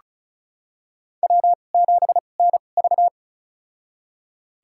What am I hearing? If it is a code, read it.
Morse code, heard as a keyed tone, about 36 wpm, spaced as for 25 wpm, 700 Hz. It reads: W7NV
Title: USA Call Signs - 4 Characters - 2x Character Spacing - 35wpm